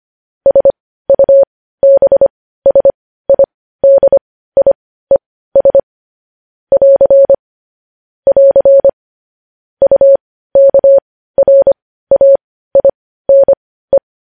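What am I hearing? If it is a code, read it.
Morse code, heard as a keyed tone, about 25 wpm, spaced as for 15 wpm, 550 Hz. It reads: SUBSIDIES <AR> <AR> UKRAINE
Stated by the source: Morse practice sending news headlines at 15 wpm.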